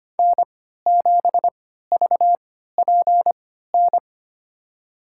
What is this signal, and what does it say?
Morse code, heard as a keyed tone, about 25 wpm, spaced as for 14 wpm, 700 Hz. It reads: N7VPN
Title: USA Call Signs - 5 Characters - 3x Character Spacing - 25wpm